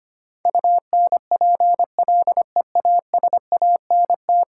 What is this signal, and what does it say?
Morse code, heard as a keyed tone, about 25 wpm, 700 Hz. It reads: UNPLEASANT
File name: 10 Letter Words - First 3 Letters Given - 25wpm